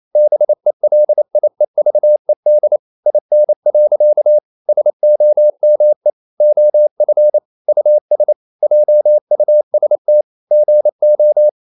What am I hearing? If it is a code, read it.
Morse code, heard as a keyed tone, about 28 wpm, 600 Hz. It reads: BELIEVED IN. SOME OF US JUST GO
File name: CW / Practice / EnglishQuotations0128